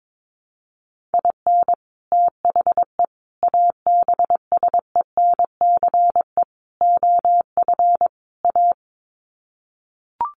Morse code, heard as a keyed tone, about 22 wpm, 700 Hz. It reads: IN THE ABSENCE OF A
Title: Sets of 5 Words Encore 22wpm